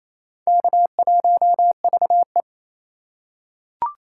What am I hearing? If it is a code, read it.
Morse code, heard as a keyed tone, about 28 wpm, 700 Hz. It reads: K1VE
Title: USA Call Signs - 4 Characters - 4x Character Spacing - 28wpm